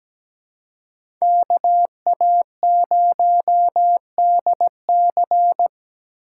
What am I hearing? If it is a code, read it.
Morse code, heard as a keyed tone, about 17 wpm, 700 Hz. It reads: KA0DC
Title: USA Call Signs - 5 Characters - 2x Character Spacing - 17wpm